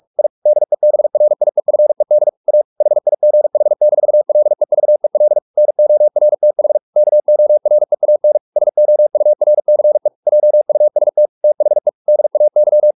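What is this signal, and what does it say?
Morse code, heard as a keyed tone, about 44 wpm, 600 Hz. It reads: I DEBRIEFED A HIGH-LEVEL NORTH KOREAN SOURCE JUST THE DAY
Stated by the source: Morse practice at 45 wpm, the novel Never by Ken Follett in English, from CW med Gnister.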